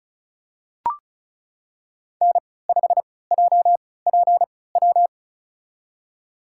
Morse code, heard as a keyed tone, about 36 wpm, spaced as for 20 wpm, 700 Hz. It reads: N5JPW
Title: USA Call Signs - 5 Characters - 3x Character Spacing - 35wpm